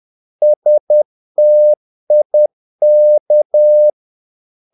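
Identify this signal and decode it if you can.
Morse code, heard as a keyed tone, about 10 wpm, 600 Hz. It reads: STIK
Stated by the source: Morse practice of short Danish words at 10 wpm, from CW med Gnister.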